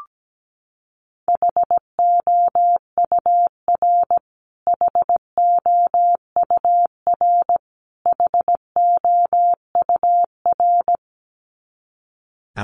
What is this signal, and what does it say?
Morse code, heard as a keyed tone, about 17 wpm, 700 Hz. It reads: HOUR HOUR HOUR
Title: Top 500 Words Repeated 17wpm